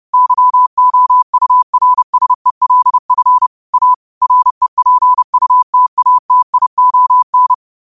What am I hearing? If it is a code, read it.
Morse code, heard as a keyed tone, about 31 wpm, 1000 Hz. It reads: YOURSELF A REPUTATION